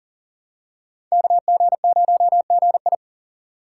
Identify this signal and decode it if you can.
Morse code, heard as a keyed tone, about 37 wpm, 700 Hz. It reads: KG0GI
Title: USA Call Signs - 6x Character Spacing - 40wpm